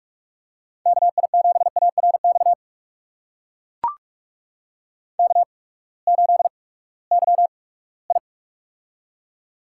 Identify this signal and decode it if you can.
Morse code, heard as a keyed tone, about 44 wpm, spaced as for 22 wpm, 700 Hz. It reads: KI7ARX K8YI
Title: USA Call Signs - 8x Character Spacing - 45wpm